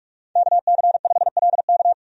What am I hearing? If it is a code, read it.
Morse code, heard as a keyed tone, about 46 wpm, 700 Hz. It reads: KC5LK
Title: USA Call Signs - 5 Characters - 45wpm